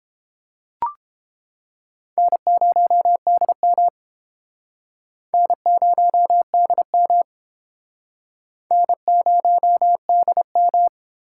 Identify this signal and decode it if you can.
Morse code, heard as a keyed tone, about 30 wpm, 700 Hz. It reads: N0DM N0DM N0DM